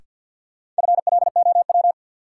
Morse code, heard as a keyed tone, about 52 wpm, 700 Hz. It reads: FLOW
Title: Sentences from Top 1200 Words - New Word Drill Alt - 50wpm